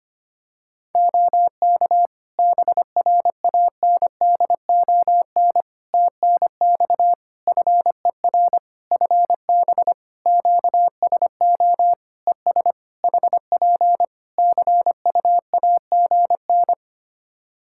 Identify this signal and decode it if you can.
Morse code, heard as a keyed tone, about 25 wpm, 700 Hz. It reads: OK BRANDON TNX FER FB QSO ES HP CUAGN
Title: K1USN SST QSO Practice 25wpm